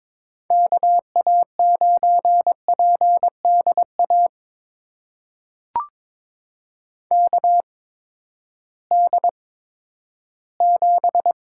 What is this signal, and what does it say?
Morse code, heard as a keyed tone, about 22 wpm, spaced as for 12 wpm, 700 Hz. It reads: KA9PDA KD7
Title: USA Call Signs - 6 Characters - 8x Character Spacing - 22wpm